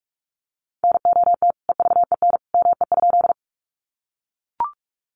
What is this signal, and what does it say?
Morse code, heard as a keyed tone, about 44 wpm, 700 Hz. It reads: NOT EVEN ME?
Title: Sentences from Top 600 Words 45wpm